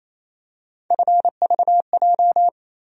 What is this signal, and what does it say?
Morse code, heard as a keyed tone, about 28 wpm, 700 Hz. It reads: FVJ
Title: USA Call Sign Postfixes - 3 Letters 28wpm